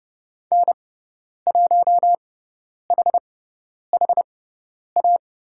Morse code, heard as a keyed tone, about 30 wpm, spaced as for 10 wpm, 700 Hz. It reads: N1HHA
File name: Lesson 16g - Letter c - Call Signs 30wpm